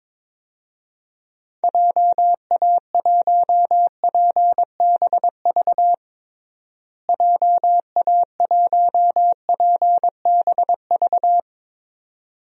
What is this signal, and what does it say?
Morse code, heard as a keyed tone, about 22 wpm, 700 Hz. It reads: JA1PBV JA1PBV
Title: DXCC-100 Call Signs - 6 Characters - Repeated 2x - 22wpm